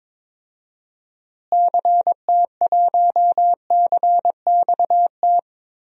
Morse code, heard as a keyed tone, about 22 wpm, 700 Hz. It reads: CT1CXT
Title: DXCC-100 Call Signs - 6 Characters - Repeated 3x - 22wpm